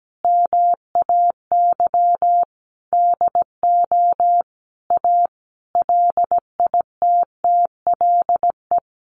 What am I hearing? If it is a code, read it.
Morse code, heard as a keyed tone, about 17 wpm, 700 Hz. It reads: MAY DO A LITTLE
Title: Sentences from Top 900 Words 17wpm